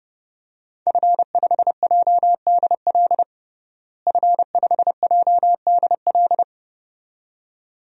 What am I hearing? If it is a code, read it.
Morse code, heard as a keyed tone, about 30 wpm, 700 Hz. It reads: F5JDL F5JDL
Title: DXCC-100 Call Signs - 5 Characters - Repeated 2x - 30wpm